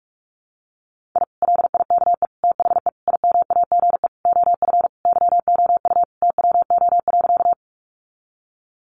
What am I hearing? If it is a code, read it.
Morse code, heard as a keyed tone, about 44 wpm, 700 Hz. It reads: I LIKE THE IMAGE OF YOU TWO.
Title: Sentences from Top 700 Words 45wpm